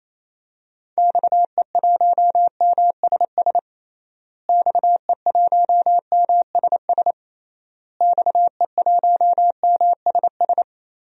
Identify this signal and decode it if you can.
Morse code, heard as a keyed tone, about 28 wpm, 700 Hz. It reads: XE1MSS XE1MSS XE1MSS
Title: DXCC-100 Call Signs - 6 Characters - Repeated 3x - 28wpm